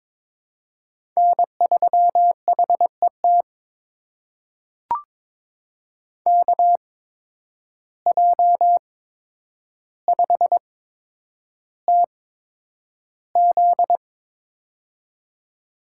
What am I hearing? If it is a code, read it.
Morse code, heard as a keyed tone, about 22 wpm, spaced as for 9 wpm, 700 Hz. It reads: N3HET KJ5TZ
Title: USA Call Signs - 5 Characters - 8x Character Spacing - 22wpm